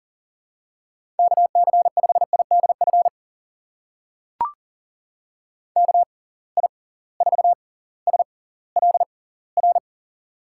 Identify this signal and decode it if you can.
Morse code, heard as a keyed tone, about 40 wpm, spaced as for 21 wpm, 700 Hz. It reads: KC5IDF KI4SLR
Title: USA Call Signs - 6 Characters - 6x Character Spacing - 40wpm